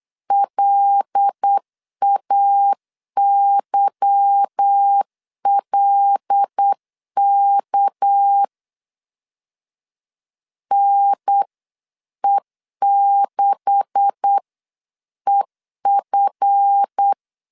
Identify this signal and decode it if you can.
Morse code, heard as a keyed tone, about 8 wpm, 800 Hz. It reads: LAYLK NE6EF
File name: CW / DET / 1 9 1500C S B 8wpm 5624